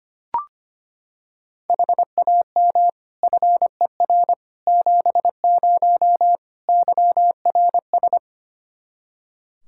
Morse code, heard as a keyed tone, about 25 wpm, 700 Hz. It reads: HAM FER 70 YRS
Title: Practice QSO - Exchange 3 - More Info - 25wpm